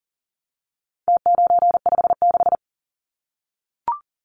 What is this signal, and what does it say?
Morse code, heard as a keyed tone, about 40 wpm, 700 Hz. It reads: T956
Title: Numbers - Contest Style - 4-Digit 40wpm